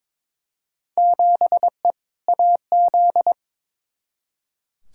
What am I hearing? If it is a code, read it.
Morse code, heard as a keyed tone, about 22 wpm, 700 Hz. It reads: 7E AZ
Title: Field Day Exchanges Encore - 22wpm